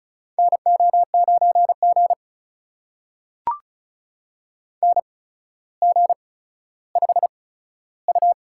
Morse code, heard as a keyed tone, about 36 wpm, spaced as for 15 wpm, 700 Hz. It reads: NO9G NG5U
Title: USA Call Signs - 4 Characters - 8x Character Spacing - 35wpm